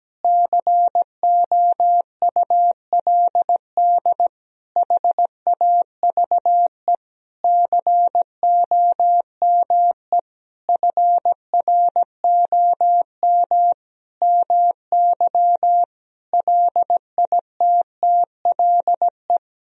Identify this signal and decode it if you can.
Morse code, heard as a keyed tone, about 17 wpm, 700 Hz. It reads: COULD HAVE COME FROM MY LITTLE
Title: Sentences from Top 1000 Words - Review Part 4 - 17wpm